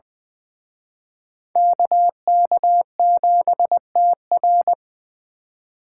Morse code, heard as a keyed tone, about 23 wpm, 700 Hz. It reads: KK7TR
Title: Speed Racing - USA 2x2 Call Signs 20wpm